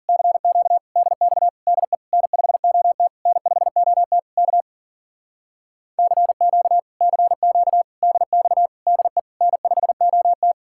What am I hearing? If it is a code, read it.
Morse code, heard as a keyed tone, about 50 wpm, 700 Hz. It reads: CQ DX DE N5OT N5OT K CQ CQ DX DE N5OT
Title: Speed-Racing Common QSO Phrases - 40wpm